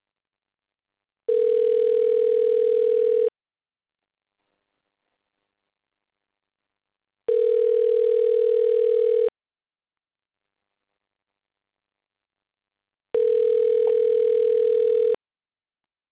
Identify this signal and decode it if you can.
Ringback tone (North America), 440 + 480 Hz.